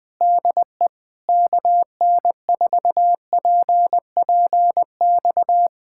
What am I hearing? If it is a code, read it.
Morse code, heard as a keyed tone, about 20 wpm, 700 Hz. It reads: DE KN4PPX
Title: Speed-Racing Common QSO Phrases - 17wpm